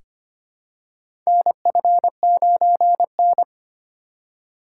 Morse code, heard as a keyed tone, about 25 wpm, 700 Hz. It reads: NF9N
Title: USA Call Signs - 4 Characters - 4x Character Spacing - 25wpm